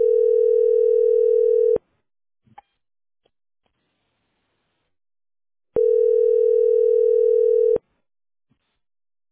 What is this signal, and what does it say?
Ringback tone (North America), 440 + 480 Hz.